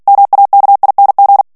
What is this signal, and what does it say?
Morse code, heard as a keyed tone, about 48 wpm, 800 Hz. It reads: MAKING
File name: ANGLAIS texte clair 3 vitesse 45 mots minute